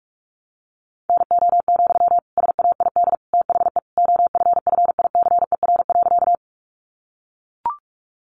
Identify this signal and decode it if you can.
Morse code, heard as a keyed tone, about 44 wpm, 700 Hz. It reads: NO, SAID THE OFFICER.
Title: Sentences from Top 1200 Words 45wpm